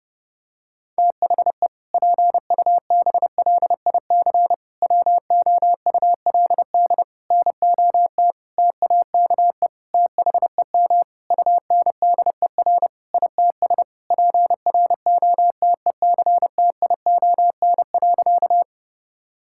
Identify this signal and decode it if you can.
Morse code, heard as a keyed tone, about 30 wpm, 700 Hz. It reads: THE PUBLIC WOULD NOT TAKE THEM UNDER ITS PROTECTION.
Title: Sentences from Top 1600 Words 30wpm